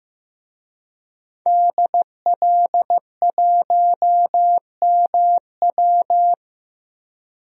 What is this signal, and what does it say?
Morse code, heard as a keyed tone, about 15 wpm, 700 Hz. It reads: DL1MW
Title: DXCC-100 Call Signs - 5 Characters - 15wpm